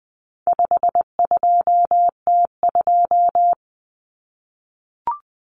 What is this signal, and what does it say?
Morse code, heard as a keyed tone, about 20 wpm, 700 Hz. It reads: 52T2
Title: Common Cut Numbers - 4-Digit 20wpm